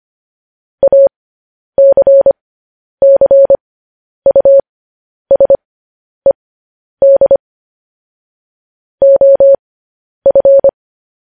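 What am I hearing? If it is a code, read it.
Morse code, heard as a keyed tone, about 25 wpm, spaced as for 10 wpm, 550 Hz. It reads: ACCUSED OF
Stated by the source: Morse code news headlines sent at 10 wpm.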